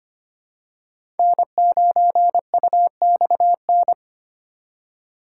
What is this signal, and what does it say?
Morse code, heard as a keyed tone, about 25 wpm, 700 Hz. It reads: N9UXN
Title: USA Call Signs - 5 Characters - 3x Character Spacing - 25wpm